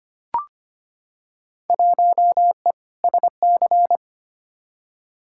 Morse code, heard as a keyed tone, about 25 wpm, 700 Hz. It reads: 1E SC
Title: Field Day Exchanges - 25wpm